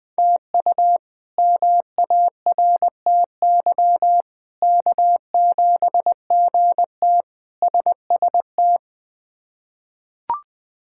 Morse code, heard as a keyed tone, about 20 wpm, 700 Hz. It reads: TU MARTY K7GT SST
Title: K1USN SST QSO Practice - 20wpm